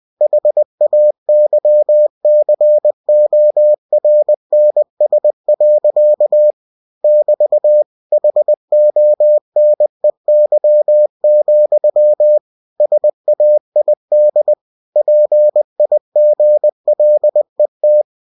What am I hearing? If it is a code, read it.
Morse code, heard as a keyed tone, about 20 wpm, 600 Hz. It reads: HAYCORNS. <BT> HONEY, SAID PIGLET
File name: cw-books / winnie-the-pooh / winnie-20wpm-page-0037